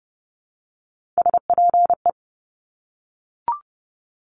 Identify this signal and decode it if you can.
Morse code, heard as a keyed tone, about 30 wpm, 690 Hz. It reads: SPE
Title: Top Three Letter Combinations in Words 30wpm